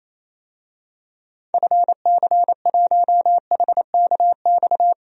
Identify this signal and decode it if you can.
Morse code, heard as a keyed tone, about 28 wpm, 700 Hz. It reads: FC1HKX
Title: DXCC-100 Call Signs - 6 Characters - 28wpm